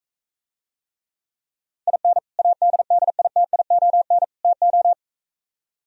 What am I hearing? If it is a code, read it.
Morse code, heard as a keyed tone, about 45 wpm, 700 Hz. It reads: IN ADDITION TO